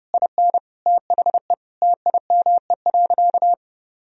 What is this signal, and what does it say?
Morse code, heard as a keyed tone, about 30 wpm, 700 Hz. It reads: IN THE TIME.
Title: Sentences from Top 200 Words and Abbreviations 30wpm